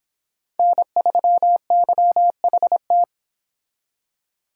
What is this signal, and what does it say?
Morse code, heard as a keyed tone, about 30 wpm, 700 Hz. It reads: N3YHT